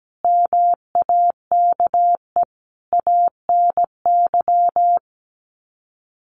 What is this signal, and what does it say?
Morse code, heard as a keyed tone, about 17 wpm, 700 Hz. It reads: MAKE ANY
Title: Sets of 5 Words Encore2 17wpm